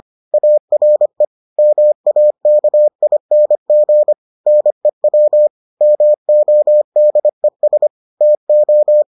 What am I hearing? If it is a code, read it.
Morse code, heard as a keyed tone, about 25 wpm, 600 Hz. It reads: ARE MAKING NEW MODES TO